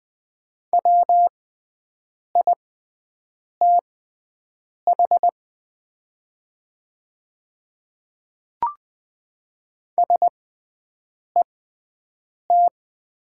Morse code, heard as a keyed tone, about 20 wpm, spaced as for 7 wpm, 700 Hz. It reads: WITH SET
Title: 6x Character Spacing - Top 300 Words - 20wpm